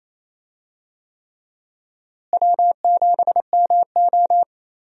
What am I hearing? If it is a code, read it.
Morse code, heard as a keyed tone, about 28 wpm, 700 Hz. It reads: W7MO
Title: USA Call Signs - 4 Characters - Repeated 3x - 28wpm